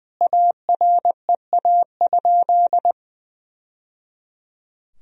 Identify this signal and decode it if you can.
Morse code, heard as a keyed tone, about 20 wpm, 700 Hz. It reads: AREA?
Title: Sentences from Top 1000 Words - Review Part 3 - 20wpm